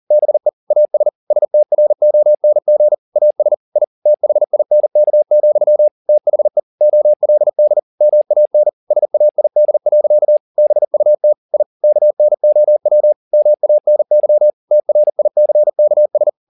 Morse code, heard as a keyed tone, about 40 wpm, 600 Hz. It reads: BE AS STRONG AS I THINK, THE OLD MAN SAID. BUT I KNOW MANY TRICKS